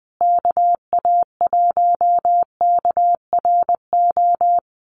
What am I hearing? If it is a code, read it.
Morse code, heard as a keyed tone, about 20 wpm, 700 Hz. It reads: KA1KRO
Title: USA Call Signs Repeated 2x 20wpm